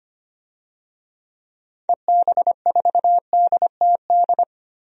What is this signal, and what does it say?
Morse code, heard as a keyed tone, about 25 wpm, 700 Hz. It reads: EB4DTD
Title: DXCC-100 Call Signs - 6 Characters - Repeated 3x - 25wpm